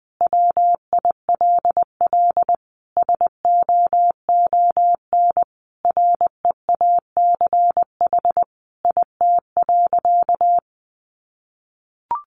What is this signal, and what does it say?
Morse code, heard as a keyed tone, about 20 wpm, 700 Hz. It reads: WILL SOON REACH IT.
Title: Sentences from Top 400 Words 20wpm